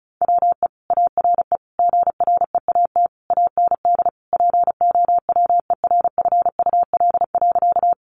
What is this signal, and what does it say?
Morse code, heard as a keyed tone, about 34 wpm, 700 Hz. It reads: WE ARE GREAT AND POWERFUL.
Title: Sentences from Top 1300 Words 35wpm